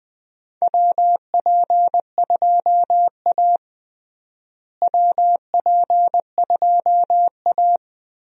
Morse code, heard as a keyed tone, about 20 wpm, 700 Hz. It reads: WP2A WP2A
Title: USA Call Signs - 4 Characters - Repeated 3x - 20wpm